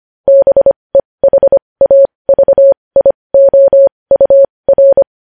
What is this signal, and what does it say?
Morse code, heard as a keyed tone, about 25 wpm, spaced as for 20 wpm, 550 Hz. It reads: BEHAVIOUR
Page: News Headlines in Morse Code at 20 WPM